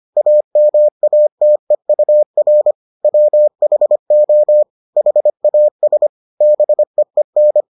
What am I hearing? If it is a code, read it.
Morse code, heard as a keyed tone, about 25 wpm, 600 Hz. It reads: AMATEUR WHO HAS BEEN